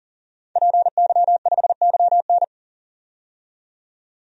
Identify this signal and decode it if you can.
Morse code, heard as a keyed tone, about 40 wpm, 700 Hz. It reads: PY5YN
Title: DXCC-100 Call Signs - 5 Characters - 40wpm